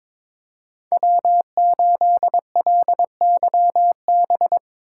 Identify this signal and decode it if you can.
Morse code, heard as a keyed tone, about 22 wpm, 700 Hz. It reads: W8LYB
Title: USA Call Signs - 5 Characters - 2x Character Spacing - 22wpm